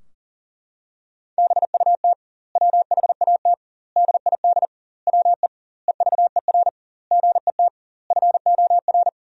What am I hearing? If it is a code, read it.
Morse code, heard as a keyed tone, about 40 wpm, 700 Hz. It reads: BUT WHAT DID WE EVER GET FOR
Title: 2x Word-Spacing - Sentences from Top 400 Words - 40wpm